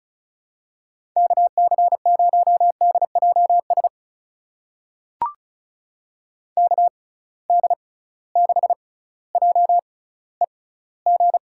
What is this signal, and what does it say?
Morse code, heard as a keyed tone, about 35 wpm, spaced as for 18 wpm, 700 Hz. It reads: KC0DJS KD6JEG
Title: USA Call Signs - 6 Characters - 6x Character Spacing - 35wpm